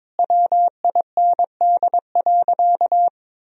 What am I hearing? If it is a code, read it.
Morse code, heard as a keyed tone, about 22 wpm, 700 Hz. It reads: WIND.
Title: Sentences from Top 1000 Words - Review Part 3 - 22wpm